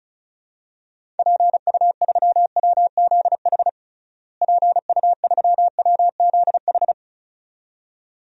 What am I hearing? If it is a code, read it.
Morse code, heard as a keyed tone, about 34 wpm, 700 Hz. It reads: PU3WZH PU3WZH
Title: DXCC-100 Call Signs - 6 Characters - Repeated 2x - 35wpm